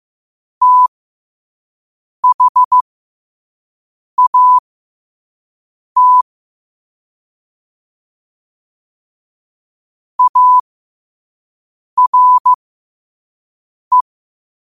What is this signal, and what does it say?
Morse code, heard as a keyed tone, about 15 wpm, spaced as for 5 wpm, 1000 Hz. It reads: THAT ARE
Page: New quotes every day in morse code at 5 Words per minute.